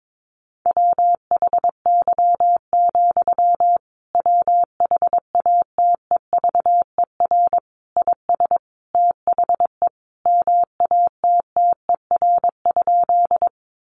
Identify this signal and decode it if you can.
Morse code, heard as a keyed tone, about 22 wpm, 700 Hz. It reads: WHY, WHATEVER IS THE MATTER?
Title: Sentences from Top 1100 Words 22wpm